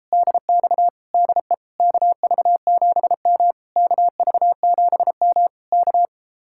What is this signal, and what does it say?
Morse code, heard as a keyed tone, about 37 wpm, 700 Hz. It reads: DX DE KV7M KV7M K